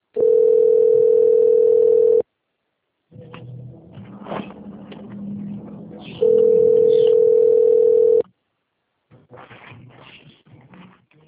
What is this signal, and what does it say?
Ringback tone (North America), 440 + 480 Hz.
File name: rg-1600-unknown- - - .2335